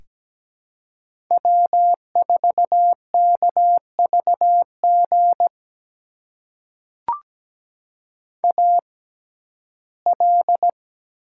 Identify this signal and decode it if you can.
Morse code, heard as a keyed tone, about 17 wpm, spaced as for 12 wpm, 700 Hz. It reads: W4KVG AL
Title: USA Call Signs - 5 Characters - 6x Character Spacing - 17wpm